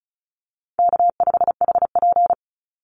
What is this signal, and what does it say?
Morse code, heard as a keyed tone, about 34 wpm, 700 Hz. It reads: K5HP
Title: Call Signs Repeated 3x - 35wpm